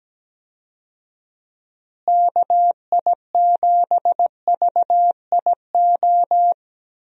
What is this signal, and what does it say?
Morse code, heard as a keyed tone, about 17 wpm, 700 Hz. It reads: KI7VIO
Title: USA Call Signs - 6 Characters - Repeated 3x - 17wpm